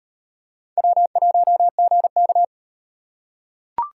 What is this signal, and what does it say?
Morse code, heard as a keyed tone, about 38 wpm, 700 Hz. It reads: W1GK